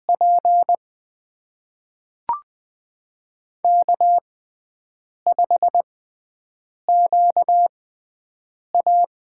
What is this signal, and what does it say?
Morse code, heard as a keyed tone, about 20 wpm, spaced as for 7 wpm, 700 Hz. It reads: P K5QA
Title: USA Call Signs - 6x Character Spacing - 20wpm